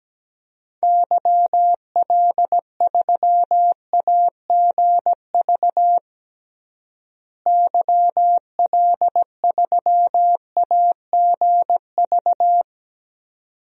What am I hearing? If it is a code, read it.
Morse code, heard as a keyed tone, about 17 wpm, 700 Hz. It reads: YL3AGV YL3AGV
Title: DXCC-100 Call Signs - 6 Characters - Repeated 2x - 17wpm